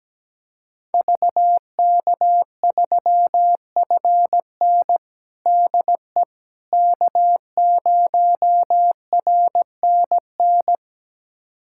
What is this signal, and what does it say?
Morse code, heard as a keyed tone, about 17 wpm, 700 Hz. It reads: VK3FN DE K0RNN
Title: USA Callsign DE Callsign - 5 Characters - 17wpm